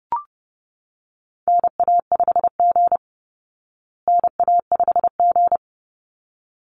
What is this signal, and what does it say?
Morse code, heard as a keyed tone, about 30 wpm, 700 Hz. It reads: NA5G NA5G
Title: Call Signs Repeated 3x - 30wpm